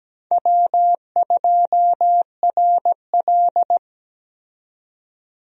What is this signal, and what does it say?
Morse code, heard as a keyed tone, about 17 wpm, 700 Hz. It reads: W2RL
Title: USA Call Signs - 4 Characters - 17wpm